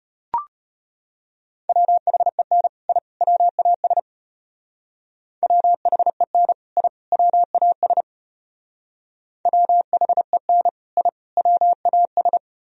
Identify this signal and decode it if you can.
Morse code, heard as a keyed tone, about 32 wpm, 700 Hz. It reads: WHEN I WAS WHEN I WAS WHEN I WAS